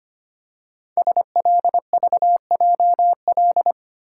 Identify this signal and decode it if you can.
Morse code, heard as a keyed tone, about 25 wpm, 700 Hz. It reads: SLVJL
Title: Code Groups - 5 Letters - 25wpm